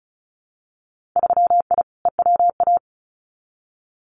Morse code, heard as a keyed tone, about 34 wpm, 700 Hz. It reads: 3I EWA
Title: Winter Field Day Exchanges Encore 35wpm